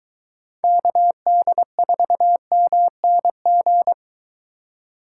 Morse code, heard as a keyed tone, about 22 wpm, 700 Hz. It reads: KD4MNG